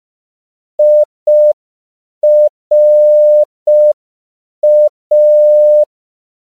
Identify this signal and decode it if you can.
Morse code, heard as a keyed tone, about 5 wpm, 600 Hz. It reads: IRA
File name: CW / 5PPM